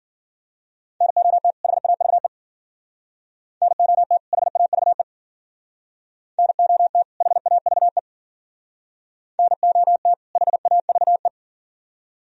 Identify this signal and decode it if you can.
Morse code, heard as a keyed tone, about 52 wpm, 700 Hz. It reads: NOT HAVE NOT HAVE NOT HAVE NOT HAVE